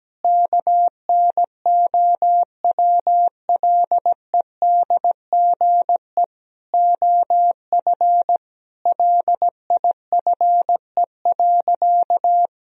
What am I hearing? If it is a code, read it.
Morse code, heard as a keyed tone, about 17 wpm, 700 Hz. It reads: KNOWLEDGE OF LIFE.
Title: Sentences from Top 1000 Words - Review Part 3 - 17wpm